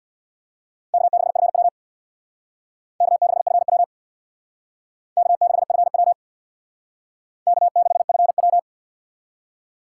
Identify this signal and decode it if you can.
Morse code, heard as a keyed tone, about 68 wpm, 700 Hz. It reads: K6FW K6FW K6FW K6FW